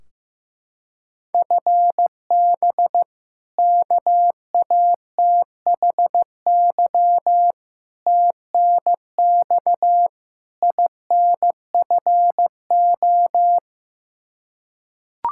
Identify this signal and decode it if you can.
Morse code, heard as a keyed tone, about 15 wpm, 700 Hz. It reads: FB KATHY TNX INFO
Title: Practice QSO - Exchange 3 - More Info - 15wpm